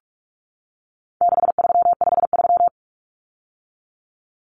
Morse code, heard as a keyed tone, about 44 wpm, 700 Hz. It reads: 6353
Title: Numbers - 4-Digit 45wpm